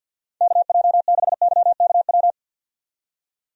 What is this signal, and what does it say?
Morse code, heard as a keyed tone, about 51 wpm, 700 Hz. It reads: KJ6YKW
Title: USA Call Signs - 6 Characters - 50wpm